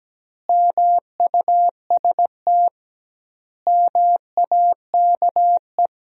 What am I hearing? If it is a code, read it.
Morse code, heard as a keyed tone, about 17 wpm, 700 Hz. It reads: MUST MAKE
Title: 2x Word-Spacing - Sentences from Top 500 Words - 17wpm